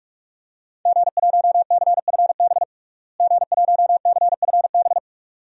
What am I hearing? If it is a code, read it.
Morse code, heard as a keyed tone, about 46 wpm, 700 Hz. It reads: G1CFB G1CFB
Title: DXCC-100 Call Signs - 5 Characters - Repeated 2x - 45wpm